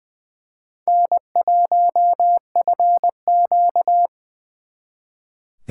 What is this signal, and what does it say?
Morse code, heard as a keyed tone, about 23 wpm, 700 Hz. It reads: N1FQ
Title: Speed Racing - USA 1x2 Call Signs 20wpm